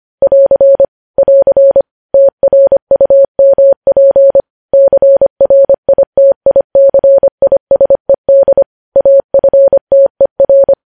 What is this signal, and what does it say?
Morse code, heard as a keyed tone, about 25 wpm, 550 Hz. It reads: <AR> <AR> TRUMP CRITICISED AFTER
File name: podcast / News headlines-1x-25x25